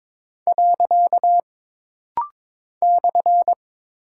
Morse code, heard as a keyed tone, about 22 wpm, 700 Hz. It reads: . /
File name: practice / 22wpm / Single Punctuation-Prosign - Rapid-Fire - 22wpm